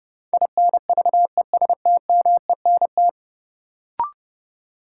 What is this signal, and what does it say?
Morse code, heard as a keyed tone, about 30 wpm, 700 Hz. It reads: INVESTMENT
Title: Sentences from Top 700 Words - New Word Drill - 30wpm